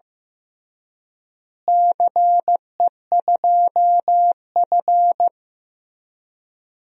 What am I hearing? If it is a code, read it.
Morse code, heard as a keyed tone, about 15 wpm, 700 Hz. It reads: CE2F
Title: DXCC-100 Call Signs - 4 Characters - Repeated 3x - 15wpm